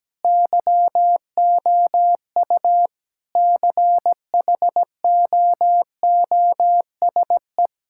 Morse code, heard as a keyed tone, about 17 wpm, 700 Hz. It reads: YOU CHOOSE
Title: Sentences from Top 1000 Words - Review Part 5 - 17wpm